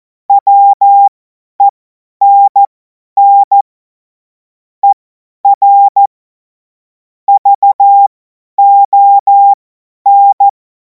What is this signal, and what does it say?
Morse code, heard as a keyed tone, about 14 wpm, spaced as for 10 wpm, 800 Hz. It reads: WENN ER VON
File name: 50 F70